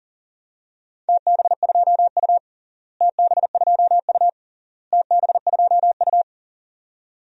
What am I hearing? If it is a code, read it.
Morse code, heard as a keyed tone, about 40 wpm, 700 Hz. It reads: TB2U TB2U TB2U
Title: DXCC-100 Call Signs - 4 Characters - Repeated 3x - 40wpm